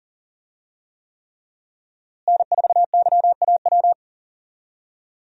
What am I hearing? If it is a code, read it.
Morse code, heard as a keyed tone, about 40 wpm, 700 Hz. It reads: N4YAW
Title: USA Call Signs - 5 Characters - Repeated 3x - 40wpm